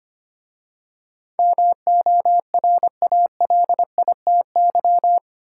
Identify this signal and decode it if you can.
Morse code, heard as a keyed tone, about 26 wpm, 690 Hz. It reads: MORALITY
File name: 8 Letter Words - First 3 Letters Given - 25wpm